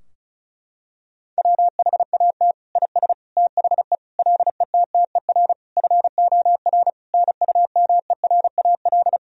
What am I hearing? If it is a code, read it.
Morse code, heard as a keyed tone, about 36 wpm, 700 Hz. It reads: WHAT IS THE LETTER FOR NUMERAL
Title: Sentences from Top 1000 Words - Review Part 2 - 35wpm